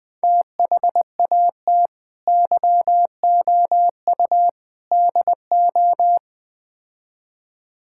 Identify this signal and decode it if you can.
Morse code, heard as a keyed tone, about 21 wpm, 700 Hz. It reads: THAT YOU DO
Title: Build A Phrase - 4-Word Phrases - 20wpm